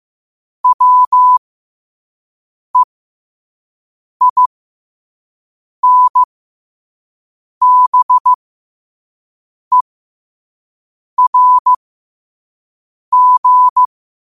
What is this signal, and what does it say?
Morse code, heard as a keyed tone, about 15 wpm, spaced as for 5 wpm, 1000 Hz. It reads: WEINBERG
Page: New quotes every day in morse code at 5 Words per minute.